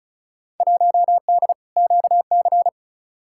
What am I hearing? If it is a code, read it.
Morse code, heard as a keyed tone, about 36 wpm, 700 Hz. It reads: 1D QC
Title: Field Day Exchanges Encore - 35wpm